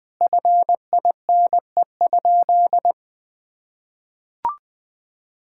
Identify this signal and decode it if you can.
Morse code, heard as a keyed tone, about 20 wpm, 700 Hz. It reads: FINE?
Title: Sentences from Top 1000 Words - Review Part 5 - 20wpm